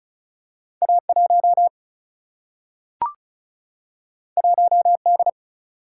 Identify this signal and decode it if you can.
Morse code, heard as a keyed tone, about 34 wpm, 700 Hz. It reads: A1 1D
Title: Single-Letter Single-Number 35wpm